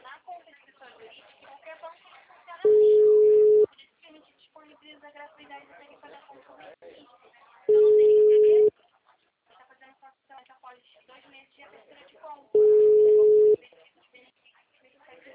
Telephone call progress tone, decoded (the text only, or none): ringback tone